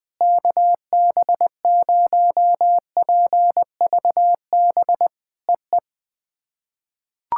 Morse code, heard as a keyed tone, about 20 wpm, 690 Hz. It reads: KB0PVB EE
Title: POTA QSO Practice - 20wpm